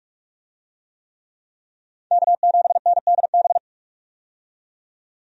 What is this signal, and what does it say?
Morse code, heard as a keyed tone, about 45 wpm, 700 Hz. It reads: K7NDB
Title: USA Call Signs - 5 Characters - 45wpm